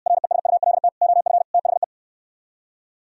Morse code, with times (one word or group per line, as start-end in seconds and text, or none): 0.1-0.9 RESULT
1.0-1.4 OF
1.5-1.8 THE